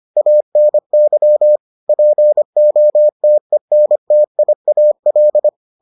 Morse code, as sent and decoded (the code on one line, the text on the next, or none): .- -. -.-- / .--. --- - . -. - .. .- .-..
ANY POTENTIAL